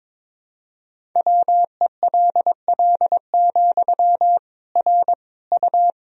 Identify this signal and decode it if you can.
Morse code, heard as a keyed tone, about 22 wpm, 700 Hz. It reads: WELL, R U
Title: Sentences from Top 400 Words and Abbreviations 22wpm